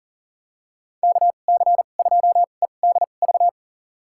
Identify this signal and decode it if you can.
Morse code, heard as a keyed tone, about 40 wpm, spaced as for 29 wpm, 700 Hz. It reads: KC2EDV
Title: USA Call Signs - 6 Characters - 2x Character Spacing - 40wpm